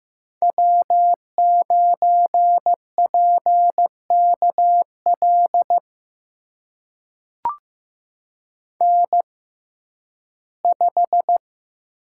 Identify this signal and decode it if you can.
Morse code, heard as a keyed tone, about 15 wpm, spaced as for 11 wpm, 700 Hz. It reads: W9PKL N5
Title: USA Call Signs - 5 Characters - 6x Character Spacing - 15wpm